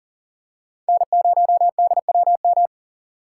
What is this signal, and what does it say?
Morse code, heard as a keyed tone, about 40 wpm, 700 Hz. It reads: N0DWM
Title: USA Call Signs - 5 Characters - 4x Character Spacing - 40wpm